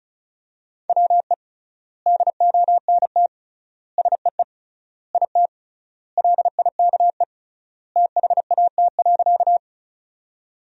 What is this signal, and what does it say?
Morse code, heard as a keyed tone, about 36 wpm, 700 Hz. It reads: WE DONT SEE IT LIKE THAT.
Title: 3x Word-Spacing - Sentences from Top 300 Words - 35wpm